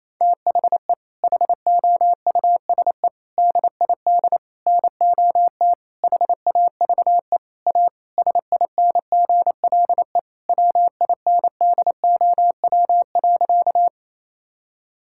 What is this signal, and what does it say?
Morse code, heard as a keyed tone, about 28 wpm, 700 Hz. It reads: THE HOUSE DID NOT HAVE A SINGLE WINDOW.
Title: Sentences from Top 1600 Words 28wpm